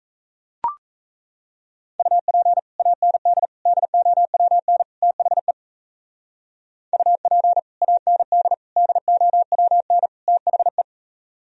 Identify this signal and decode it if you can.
Morse code, heard as a keyed tone, about 38 wpm, 700 Hz. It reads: UP AND DOWN THE UP AND DOWN THE